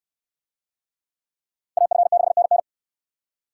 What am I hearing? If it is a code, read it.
Morse code, heard as a keyed tone, about 69 wpm, 700 Hz. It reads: AF6NN